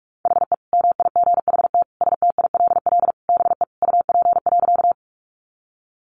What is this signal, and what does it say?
Morse code, heard as a keyed tone, about 44 wpm, 700 Hz. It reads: HE MIGHT STILL BE UP.
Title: Sentences from Top 300 Words 45wpm